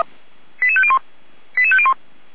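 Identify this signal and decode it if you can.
Five-tone selective call (ZVEI), heard as quick runs of five tones, tones of about 70 ms. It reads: 88501 88501
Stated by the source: ZVEI five-tone call 88190 received by SDR.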